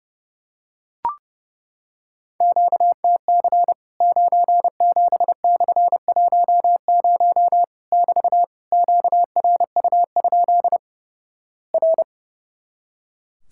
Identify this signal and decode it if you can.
Morse code, heard as a keyed tone, about 32 wpm, 700 Hz. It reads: QTC 97/10 <BT> QRU? R
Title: Worked All Europe DX Contest - QTC Exchange - 30wpm